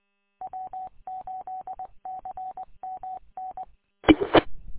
Morse code, heard as a keyed tone, about 20 wpm, 750 Hz. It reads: W8CMN E E
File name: w8cmn- -combined